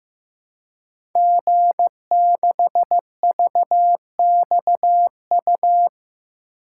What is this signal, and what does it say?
Morse code, heard as a keyed tone, about 15 wpm, 700 Hz. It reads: G6VXU
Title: Code Groups - 5 Letter:Numbers - 15wpm